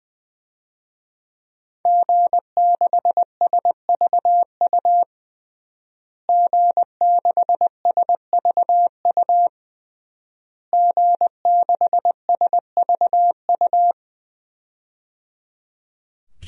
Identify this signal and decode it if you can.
Morse code, heard as a keyed tone, about 21 wpm, 700 Hz. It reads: G6SVU G6SVU G6SVU
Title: DXCC-100 Call Signs - 5 Characters - Repeated 3x - 20wpm